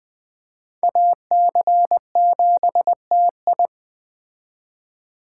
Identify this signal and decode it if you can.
Morse code, heard as a keyed tone, about 21 wpm, 700 Hz. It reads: AC7TI